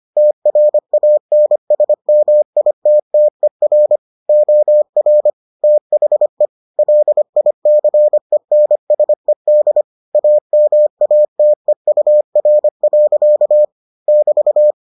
Morse code, heard as a keyed tone, about 25 wpm, 600 Hz. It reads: TRANSMITTER OR THE LICENSED AMATEUR. <BT>